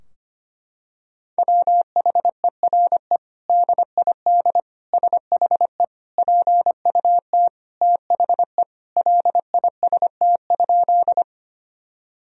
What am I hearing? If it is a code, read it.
Morse code, heard as a keyed tone, about 25 wpm, 700 Hz. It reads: WHERE DID SHE PUT THE LIST?
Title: Sentences from Top 1000 Words - Review Part 2 - 25wpm